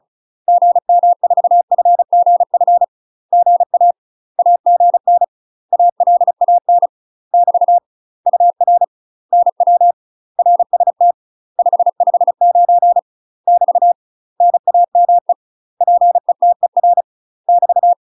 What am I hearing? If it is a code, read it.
Morse code, heard as a keyed tone, about 34 wpm, 700 Hz. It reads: GM4FGF GA AGN ALAN <BT> UR NW RST 559 <BT> NAME PETER <BT>
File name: UK MASTER QSOs TRAINING / 35WPM / UK MASTER QSOs 0006